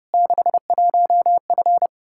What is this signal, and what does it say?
Morse code, heard as a keyed tone, about 30 wpm, 700 Hz. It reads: 61F
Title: Practice QSO - Exchange 2 - Info - 30wpm